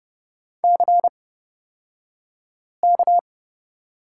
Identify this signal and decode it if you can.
Morse code, heard as a keyed tone, about 30 wpm, spaced as for 5 wpm, 700 Hz. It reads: CK
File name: Lesson 38d - Letter q - Two Characters 30wpm